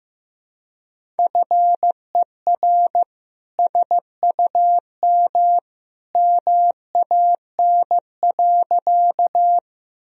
Morse code, heard as a keyed tone, about 15 wpm, 700 Hz. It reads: FER SUM MAN.
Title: Sentences from Top 200 Words and Abbreviations 15wpm